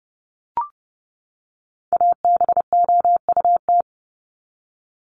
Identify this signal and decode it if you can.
Morse code, heard as a keyed tone, about 30 wpm, 700 Hz. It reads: ABOUT
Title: Top 500 Words 30wpm